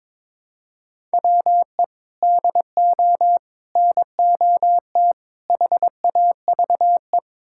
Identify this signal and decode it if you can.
Morse code, heard as a keyed tone, about 21 wpm, 700 Hz. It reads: WE DO NOT HAVE
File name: practice / 22wpm / Build A Phrase - 5-Word Phrases - 22wpm